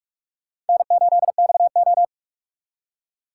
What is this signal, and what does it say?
Morse code, heard as a keyed tone, about 58 wpm, 700 Hz. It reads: N8XO